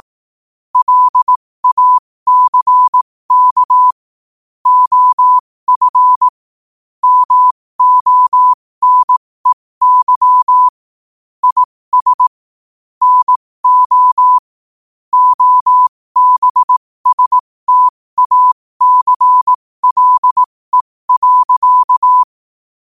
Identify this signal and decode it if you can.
Morse code, heard as a keyed tone, about 18 wpm, spaced as for 15 wpm, 1000 Hz. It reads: LACK OF MONEY IS NO OBSTACLE.
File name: qotdfiles / 15wpm